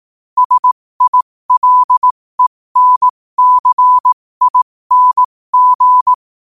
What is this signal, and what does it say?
Morse code, heard as a keyed tone, about 18 wpm, spaced as for 15 wpm, 1000 Hz. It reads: SILENCING
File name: qotdfiles / 15wpm